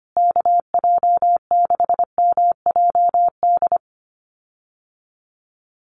Morse code, heard as a keyed tone, about 25 wpm, 700 Hz. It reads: KJ6MJD
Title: USA Call Signs Repeated 3x 25wpm